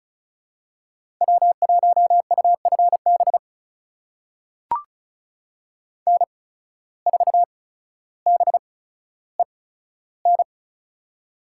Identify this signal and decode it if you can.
Morse code, heard as a keyed tone, about 34 wpm, spaced as for 15 wpm, 700 Hz. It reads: W1UFB N4BEN
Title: USA Call Signs - 5 Characters - 8x Character Spacing - 35wpm